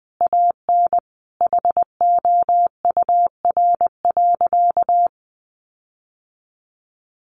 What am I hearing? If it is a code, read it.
Morse code, heard as a keyed tone, about 20 wpm, 700 Hz. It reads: AN HOUR.
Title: Sentences from Top 1300 Words 20wpm